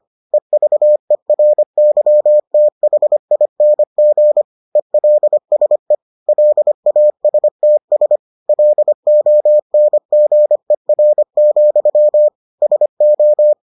Morse code, heard as a keyed tone, about 25 wpm, 600 Hz. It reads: EVERYTHING ELSE LASTS LONGER, SO